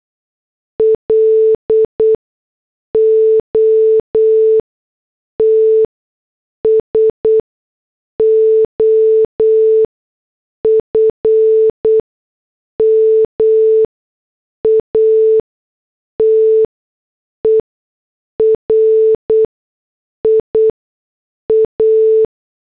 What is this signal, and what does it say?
Morse code, heard as a keyed tone, about 8 wpm, spaced as for 6 wpm, 440 Hz. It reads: LOTS OF MATERIA
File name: morse / headlines